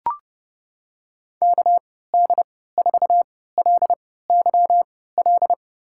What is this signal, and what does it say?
Morse code, heard as a keyed tone, about 30 wpm, spaced as for 17 wpm, 700 Hz. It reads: KD4LYL
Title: USA Call Signs - 6 Characters - 3x Character Spacing - 30wpm